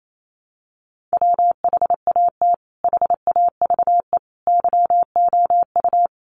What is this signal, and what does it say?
Morse code, heard as a keyed tone, about 28 wpm, 700 Hz. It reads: WHAT HAVE YOU
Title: Sentences from Top 100 Words 28wpm